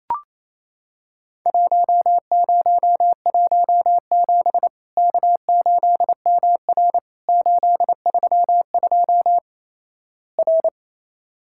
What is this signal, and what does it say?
Morse code, heard as a keyed tone, about 28 wpm, 700 Hz. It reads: 1017 K8MR 832 R
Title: Worked All Europe DX Contest - QTC Exchange - 28wpm